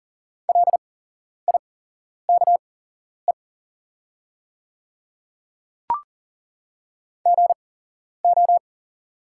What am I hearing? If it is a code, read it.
Morse code, heard as a keyed tone, about 41 wpm, spaced as for 11 wpm, 690 Hz. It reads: LIKE GO
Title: 8x Character Spacing - Top 100 Words - 40wpm